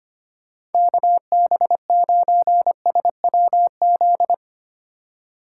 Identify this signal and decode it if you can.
Morse code, heard as a keyed tone, about 32 wpm, 700 Hz. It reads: KB9SWZ